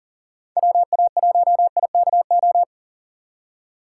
Morse code, heard as a keyed tone, about 50 wpm, 700 Hz. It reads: WA1IKO
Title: Speed Racing - USA 2x3 Call Signs 40wpm